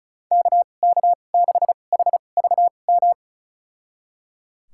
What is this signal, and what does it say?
Morse code, heard as a keyed tone, about 34 wpm, spaced as for 25 wpm, 700 Hz. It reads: KK6HVM
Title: USA Call Signs - 6 Characters - 2x Character Spacing - 35wpm